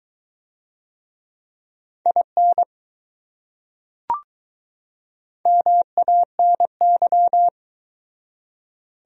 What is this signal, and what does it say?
Morse code, heard as a keyed tone, about 20 wpm, 700 Hz. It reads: IN MANY